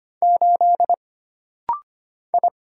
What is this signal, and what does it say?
Morse code, heard as a keyed tone, about 25 wpm, 700 Hz. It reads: 8 I
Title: Single Letter-Number - Rapid-Fire 25wpm